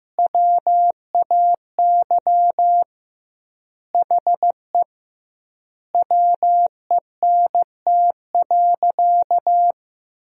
Morse code, heard as a keyed tone, about 15 wpm, 700 Hz. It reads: WAY HE WENT.
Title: 2x Word-Spacing - Sentences from Top 200 Words - 15wpm